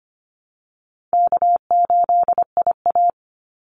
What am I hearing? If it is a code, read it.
Morse code, heard as a keyed tone, about 25 wpm, 700 Hz. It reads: K8IA
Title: Easy Call Signs - 25wpm